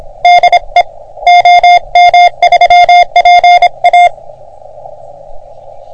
Morse code, heard as a keyed tone, about 26 wpm, 720 Hz. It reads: DE OM3PA
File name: OM3PA PILUP CUC 388